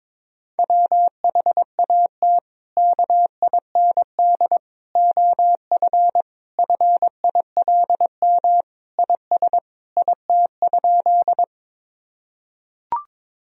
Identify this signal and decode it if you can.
Morse code, heard as a keyed tone, about 22 wpm, 700 Hz. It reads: WHAT KIND OF FILM IS IT?
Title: Sentences from Top 1000 Words - Review Part 3 - 22wpm